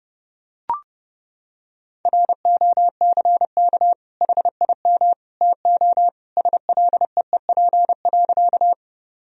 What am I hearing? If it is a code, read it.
Morse code, heard as a keyed tone, about 30 wpm, 700 Hz. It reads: ROCK HIM TO SLEEP.
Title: Sentences from Top 1000 Words - Review Part 2 - 30wpm